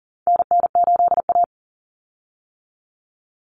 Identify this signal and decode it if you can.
Morse code, heard as a keyed tone, about 40 wpm, 700 Hz. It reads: NN8A
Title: Common Cut Numbers - 4-Digit 40wpm